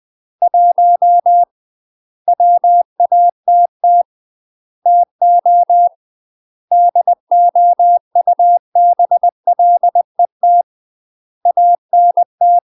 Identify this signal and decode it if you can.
Morse code, heard as a keyed tone, about 20 wpm, 700 Hz. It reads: 1 WATT TO DOUBLET ANT
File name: UK MASTER QSOs TRAINING / 20WPM / UK MASTER QSOs 0037